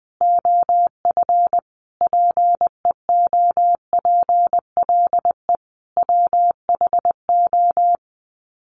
Morse code, heard as a keyed tone, about 20 wpm, 700 Hz. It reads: OF PEOPLE WHO
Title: Sets of 5 Words Encore 20wpm